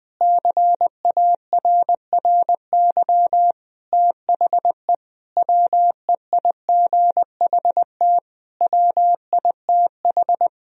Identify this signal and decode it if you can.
Morse code, heard as a keyed tone, about 20 wpm, 700 Hz. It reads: CARRY THE WEIGHT WITH
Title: Sentences from Top 1000 Words - Review Part 3 - 20wpm